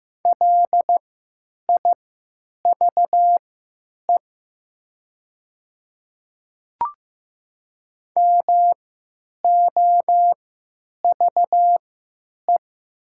Morse code, heard as a keyed tone, about 15 wpm, spaced as for 9 wpm, 700 Hz. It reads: LIVE MOVE
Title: 3x Character Spacing - Top 500 Words - 15wpm